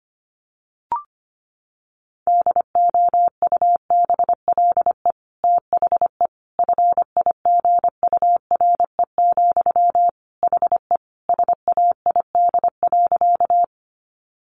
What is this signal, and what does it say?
Morse code, heard as a keyed tone, about 25 wpm, 700 Hz. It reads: DOUBLE THE FIGURE, HE SAID.
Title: Sentences from Top 1400 Words 25wpm